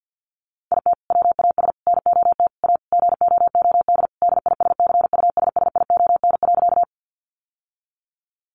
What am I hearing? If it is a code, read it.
Morse code, heard as a keyed tone, about 48 wpm, 700 Hz. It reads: IT WAS NOT A GOOD DISCUSSION.
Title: Sentences from Top 1100 Words 50wpm